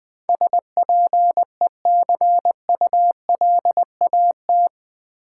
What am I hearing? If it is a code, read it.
Morse code, heard as a keyed tone, about 20 wpm, 700 Hz. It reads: SPECULAT
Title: Build A Word - 4 Letters At A Time - 11-Letter Words - 20wpm